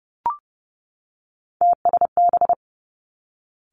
Morse code, heard as a keyed tone, about 30 wpm, 700 Hz. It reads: TSB
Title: Three Challenging Letter-Number Combinations 30wpm